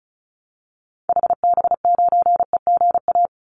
Morse code, heard as a keyed tone, about 34 wpm, 700 Hz. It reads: HB9EGA
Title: Call Signs - 35wpm